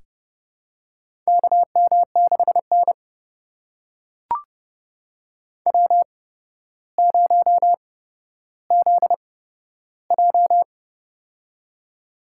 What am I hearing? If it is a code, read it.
Morse code, heard as a keyed tone, about 32 wpm, spaced as for 13 wpm, 700 Hz. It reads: KM6N W0ZJ
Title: USA Call Signs - 4 Characters - 8x Character Spacing - 30wpm